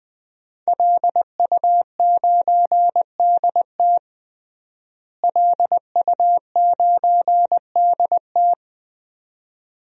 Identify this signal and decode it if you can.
Morse code, heard as a keyed tone, about 20 wpm, 700 Hz. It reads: LU9DT LU9DT
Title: DXCC-100 Call Signs - 5 Characters - Repeated 2x - 20wpm